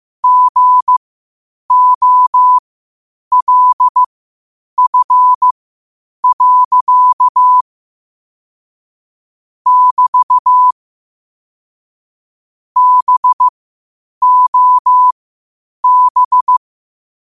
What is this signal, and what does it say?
Morse code, heard as a keyed tone, about 15 wpm, spaced as for 8 wpm, 1000 Hz. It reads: GOLF. <BT> BOB
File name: qotdfiles / 8wpm